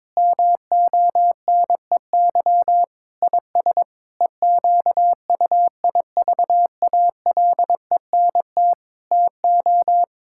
Morse code, heard as a keyed tone, about 23 wpm, 700 Hz. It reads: MONEY IS EQUIVALENT TO
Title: Sentences from Top 1600 Words 22wpm